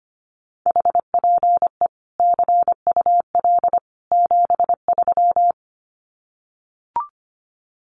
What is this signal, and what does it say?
Morse code, heard as a keyed tone, about 25 wpm, 700 Hz. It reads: HPE CUL 73
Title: Common QSO Phrases 25wpm